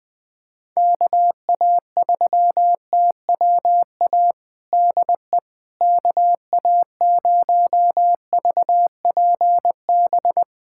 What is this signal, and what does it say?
Morse code, heard as a keyed tone, about 20 wpm, 700 Hz. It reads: KA3TWA DE KA0VPB
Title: USA Callsign DE Callsign - 6 Characters - 20wpm